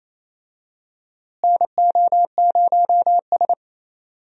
Morse code, heard as a keyed tone, about 28 wpm, 700 Hz. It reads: NO0S
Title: USA Call Signs - 4 Characters - Repeated 3x - 28wpm